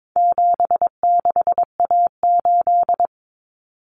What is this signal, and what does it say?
Morse code, heard as a keyed tone, about 22 wpm, 700 Hz. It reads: 76A8
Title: Common Cut Numbers - 4-Digit 22wpm